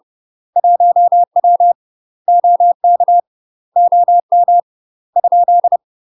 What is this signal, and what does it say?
Morse code, heard as a keyed tone, about 30 wpm, 700 Hz. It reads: 1W OK OM ?
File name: UK MASTER QSOs TRAINING / 30WPM / UK MASTER QSOs 0035